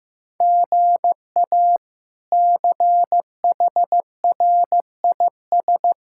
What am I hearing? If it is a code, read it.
Morse code, heard as a keyed tone, about 15 wpm, 700 Hz. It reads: GA CHRIS
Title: K1USN SST QSO Practice - 15wpm